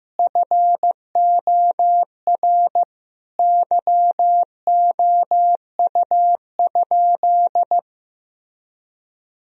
Morse code, heard as a keyed tone, about 15 wpm, 700 Hz. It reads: FOR YOU?
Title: Sentences from Top 1000 Words - Review Part 2 - 15wpm